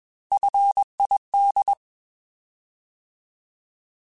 Morse code, heard as a keyed tone, about 21 wpm, 800 Hz. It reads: FID
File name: Alphabetical / Alphabetical mp3s / Fs @20